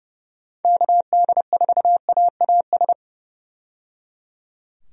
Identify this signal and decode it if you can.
Morse code, heard as a keyed tone, about 30 wpm, 700 Hz. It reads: KD4AAS
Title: USA Call Signs - 6 Characters - 30wpm